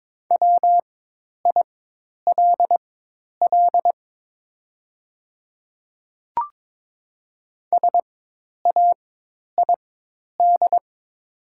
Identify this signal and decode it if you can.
Morse code, heard as a keyed tone, about 22 wpm, spaced as for 10 wpm, 700 Hz. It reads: WILL SAID
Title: 4x Character Spacing - Top 100 Words - 22wpm